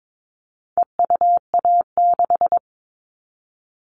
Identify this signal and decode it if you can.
Morse code, heard as a keyed tone, about 22 wpm, 700 Hz. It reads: EUA6
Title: All Cut Numbers - 4-Digit 22wpm